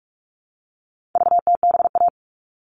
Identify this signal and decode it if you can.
Morse code, heard as a keyed tone, about 44 wpm, 700 Hz. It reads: VTBA
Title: All Cut Numbers - 4-Digit 45wpm